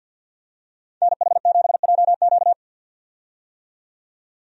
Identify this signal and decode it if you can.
Morse code, heard as a keyed tone, about 52 wpm, 700 Hz. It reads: NH7JQ
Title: USA Call Signs - 5 Characters - 50wpm